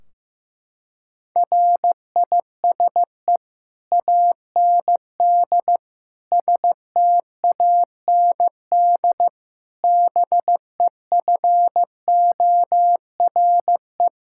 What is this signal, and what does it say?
Morse code, heard as a keyed tone, about 15 wpm, 700 Hz. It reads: RISE AND STAND BEFORE
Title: Sentences from Top 1600 Words 15wpm